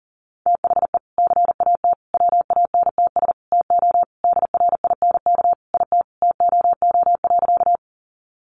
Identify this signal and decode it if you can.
Morse code, heard as a keyed tone, about 40 wpm, 700 Hz. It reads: THE CAT WANTS TO DRINK IT TOO.
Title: Additional Easy Phrases 40wpm